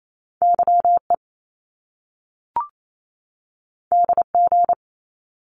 Morse code, heard as a keyed tone, about 28 wpm, 700 Hz. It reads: YE DG
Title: Top Two Letter Combinations in Words 28wpm